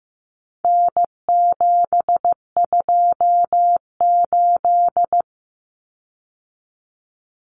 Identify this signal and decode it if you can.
Morse code, heard as a keyed tone, about 15 wpm, 700 Hz. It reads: N728
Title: Common Cut Numbers - 4-Digit 15wpm